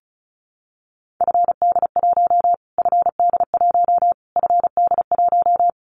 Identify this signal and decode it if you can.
Morse code, heard as a keyed tone, about 34 wpm, 700 Hz. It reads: FD1 FD1 FD1
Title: Callsign Prefixes for Top 25 DXCC - Repeated 3x - 35wpm